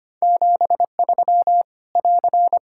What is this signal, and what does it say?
Morse code, heard as a keyed tone, about 25 wpm, 700 Hz. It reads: 73 <AR>
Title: Practice QSO - Exchange 4 - Ending - 25wpm